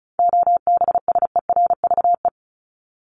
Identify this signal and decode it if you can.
Morse code, heard as a keyed tone, about 34 wpm, 700 Hz. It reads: OBSERVE
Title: ED Suffix - Top 100 Words with Base Words 35wpm